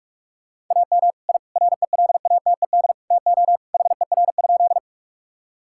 Morse code, heard as a keyed tone, about 46 wpm, 700 Hz. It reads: AM I RELATED TO HER?
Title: Sentences from Top 1000 Words - Review Part 5 - 45wpm